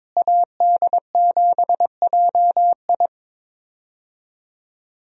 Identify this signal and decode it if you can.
Morse code, heard as a keyed tone, about 22 wpm, 700 Hz. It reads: AD7JI
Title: USA Call Signs - 5 Characters - Repeated 2x - 22wpm